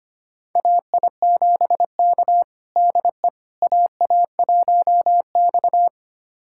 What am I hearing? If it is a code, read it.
Morse code, heard as a keyed tone, about 25 wpm, 700 Hz. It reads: AI7K DE AA1X
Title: USA Callsign DE Callsign - 4 Characters - 25wpm